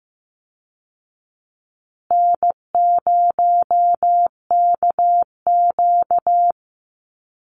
Morse code, heard as a keyed tone, about 15 wpm, 700 Hz. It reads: N0KQ
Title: Easy Call Signs Repeated 3x - 15wpm